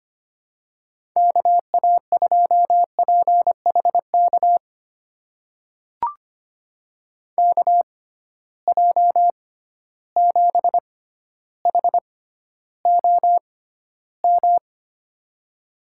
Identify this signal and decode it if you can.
Morse code, heard as a keyed tone, about 25 wpm, spaced as for 13 wpm, 700 Hz. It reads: KA2PHK KJ7HOM
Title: USA Call Signs - 6 Characters - 6x Character Spacing - 25wpm